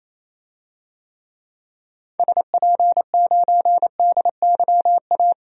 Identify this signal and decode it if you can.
Morse code, heard as a keyed tone, about 28 wpm, 700 Hz. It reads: SP9DYA
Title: DXCC-100 Call Signs - 6 Characters - 28wpm